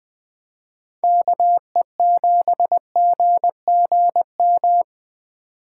Morse code, heard as a keyed tone, about 20 wpm, 700 Hz. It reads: KE7GGM
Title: Speed Racing - USA Call Signs 17wpm